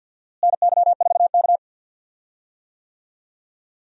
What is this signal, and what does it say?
Morse code, heard as a keyed tone, about 52 wpm, 700 Hz. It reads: NY4K
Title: USA Call Signs - 4 Characters - Repeated 2x - 50wpm